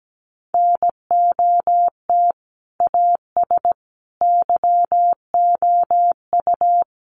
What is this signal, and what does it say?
Morse code, heard as a keyed tone, about 17 wpm, 700 Hz. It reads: NOT AS YOU
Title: Sentences from Top 1200 Words 17wpm